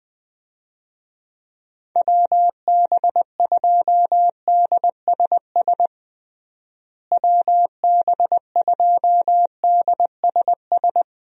Morse code, heard as a keyed tone, about 20 wpm, 700 Hz. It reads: WB2DSS WB2DSS
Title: USA Call Signs - 6 Characters - Repeated 2x - 20wpm